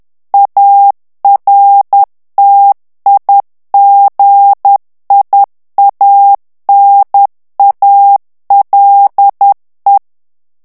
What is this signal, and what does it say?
Morse code, heard as a keyed tone, about 10 wpm, 800 Hz. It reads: ARTIGIANALE
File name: morse mp3 / ITALIEN texte clair 1 vitesse 10 mots minute